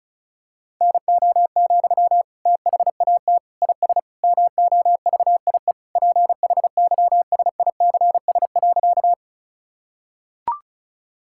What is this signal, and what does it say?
Morse code, heard as a keyed tone, about 34 wpm, 700 Hz. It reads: NO, THAT IS MOVIE PHYSICS.
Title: Sentences from Top 1000 Words - Review Part 5 - 35wpm